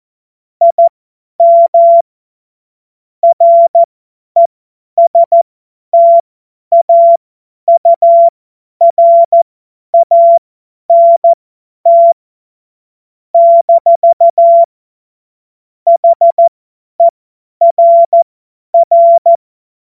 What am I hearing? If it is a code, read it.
Morse code, heard as a keyed tone, about 14 wpm, spaced as for 10 wpm, 680 Hz. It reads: IM RESTAURANT - HERR
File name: Morsekurs / Witze / Witz 25. Mai 2021, 50 F70